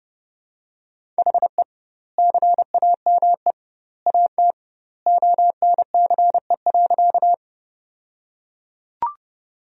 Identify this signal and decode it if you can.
Morse code, heard as a keyed tone, about 30 wpm, 700 Hz. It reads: HE CAME AT ONCE.
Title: 2x Word-Spacing - Sentences from Top 300 Words - 30wpm